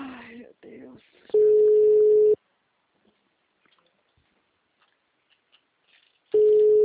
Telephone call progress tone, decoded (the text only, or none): ringback tone